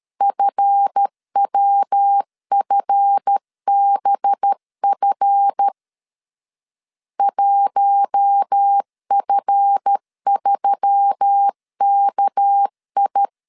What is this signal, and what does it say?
Morse code, heard as a keyed tone, about 13 wpm, 800 Hz. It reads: FWFBF 1F3KI
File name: LEZIONI / 7 GF9Z / 7 GF 12wpm 2122